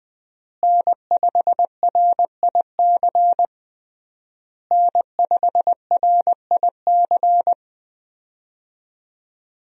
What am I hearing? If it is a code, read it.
Morse code, heard as a keyed tone, about 20 wpm, 700 Hz. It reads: N5RIC N5RIC
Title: USA Call Signs - 5 Characters - Repeated 3x - 20wpm